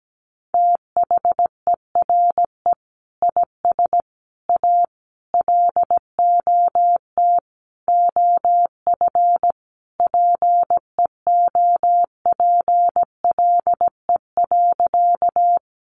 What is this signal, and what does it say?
Morse code, heard as a keyed tone, about 17 wpm, 700 Hz. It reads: THERE IS A LOT OF PEOPLE.
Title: Sentences from Top 500 Words 17wpm